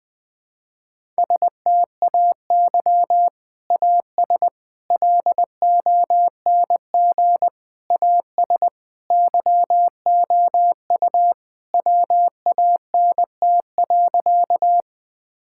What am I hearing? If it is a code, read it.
Morse code, heard as a keyed tone, about 20 wpm, 700 Hz. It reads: STAY AS LONG AS YOU WANT.
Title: Sentences with 3-Word Phrases - 20wpm